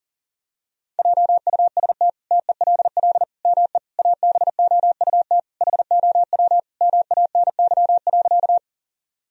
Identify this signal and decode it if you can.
Morse code, heard as a keyed tone, about 40 wpm, 700 Hz. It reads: JUST TELL ME ABOUT HOW MANY.
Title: Sentences from Top 1000 Words - Review Part 1 - 40wpm